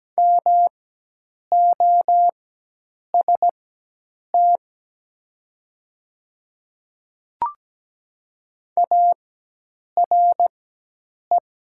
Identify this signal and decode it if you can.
Morse code, heard as a keyed tone, about 17 wpm, spaced as for 8 wpm, 700 Hz. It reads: MOST ARE
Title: 4x Character Spacing - Top 100 Words - 17wpm